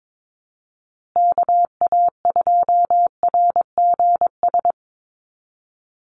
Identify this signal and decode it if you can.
Morse code, heard as a keyed tone, about 22 wpm, 700 Hz. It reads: KA2RGS
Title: USA Call Signs Repeated 3x 22wpm